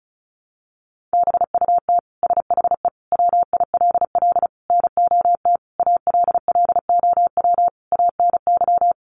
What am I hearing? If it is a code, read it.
Morse code, heard as a keyed tone, about 34 wpm, 700 Hz. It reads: BUT SHE WILL NOT ALLOW ANY
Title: Sentences from Top 1200 Words 35wpm